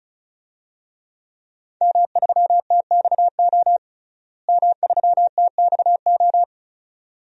Morse code, heard as a keyed tone, about 34 wpm, 700 Hz. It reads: M3TXO M3TXO
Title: DXCC-100 Call Signs - 5 Characters - Repeated 2x - 35wpm